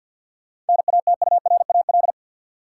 Morse code, heard as a keyed tone, about 53 wpm, 700 Hz. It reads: NATURAL
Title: Sentences from Top 700 Words - New Word Drill - 50wpm